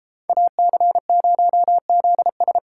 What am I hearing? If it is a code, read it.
Morse code, heard as a keyed tone, about 33 wpm, 700 Hz. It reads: AC0ZS